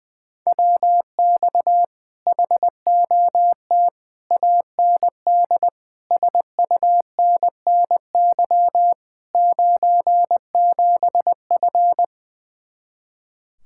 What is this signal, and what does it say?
Morse code, heard as a keyed tone, about 20 wpm, 700 Hz. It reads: WX HOT AND SUNNY 97F
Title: Speed-Racing Common QSO Phrases - 20wpm